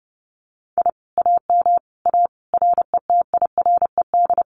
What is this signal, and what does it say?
Morse code, heard as a keyed tone, about 30 wpm, 700 Hz. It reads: I AM A RETIRED
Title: Common QSO Phrases Encore 30wpm